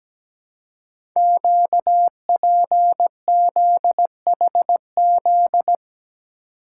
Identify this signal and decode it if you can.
Morse code, heard as a keyed tone, about 17 wpm, 700 Hz. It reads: QPZHZ
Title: Code Groups - 5 Letters - 17wpm